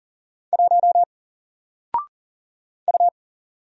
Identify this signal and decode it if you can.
Morse code, heard as a keyed tone, about 40 wpm, 700 Hz. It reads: 1 U
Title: Single Letter-Number 40wpm